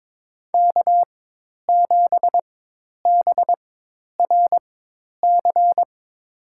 Morse code, heard as a keyed tone, about 22 wpm, spaced as for 10 wpm, 700 Hz. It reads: K7BRC
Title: USA Call Signs - 5 Characters - 4x Character Spacing - 22wpm